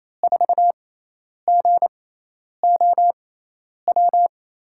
Morse code, heard as a keyed tone, about 28 wpm, spaced as for 10 wpm, 700 Hz. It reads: 4GOW
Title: USA Call Signs - 6 Characters - 6x Character Spacing - 28wpm